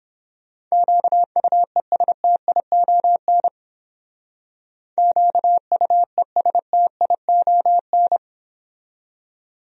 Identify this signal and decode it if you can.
Morse code, heard as a keyed tone, about 30 wpm, 700 Hz. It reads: QUESTION QUESTION